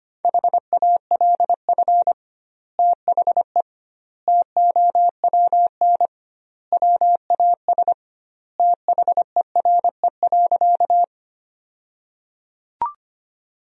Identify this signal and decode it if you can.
Morse code, heard as a keyed tone, about 25 wpm, 700 Hz. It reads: HALF THE TOWN WAS THERE.
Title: 2x Word-Spacing - Sentences from Top 500 Words - 25wpm